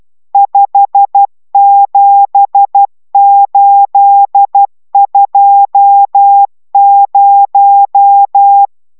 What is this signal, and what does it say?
Morse code, heard as a keyed tone, about 12 wpm, 800 Hz. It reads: 57820